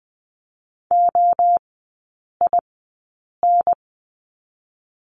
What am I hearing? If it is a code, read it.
Morse code, heard as a keyed tone, about 20 wpm, 700 Hz. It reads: O I N
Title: 029 Lesson 6e - Letter i - Three Characters 20wpm